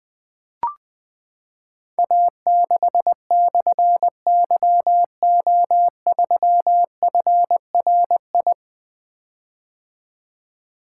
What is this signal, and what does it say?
Morse code, heard as a keyed tone, about 20 wpm, 700 Hz. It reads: A6/YO3FRI
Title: Difficult Call Signs 20wpm